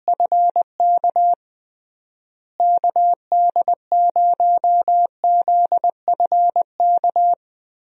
Morse code, heard as a keyed tone, about 20 wpm, 700 Hz. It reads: FK KD0ZFK
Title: USA Call Signs - 6 Characters - Repeated 3x - 20wpm